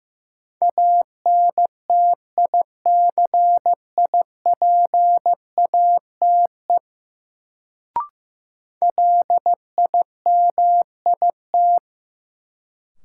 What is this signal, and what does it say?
Morse code, heard as a keyed tone, about 15 wpm, 700 Hz. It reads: ANTICIPATE LIMIT
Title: Sentences from Top 1500 Words - New Word Drill - 15wpm